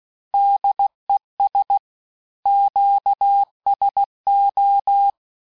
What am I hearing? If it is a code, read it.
Morse code, heard as a keyed tone, about 16 wpm, 800 Hz. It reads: DES QSO
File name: cours-du-lundi files / cours-1580-fin-QSO